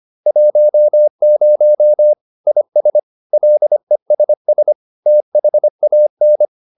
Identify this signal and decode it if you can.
Morse code, heard as a keyed tone, about 25 wpm, 600 Hz. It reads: 10 IS LESS THAN